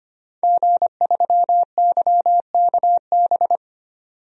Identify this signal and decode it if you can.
Morse code, heard as a keyed tone, about 25 wpm, 700 Hz. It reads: G3YKB
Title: DXCC-100 Call Signs - 5 Characters - 25wpm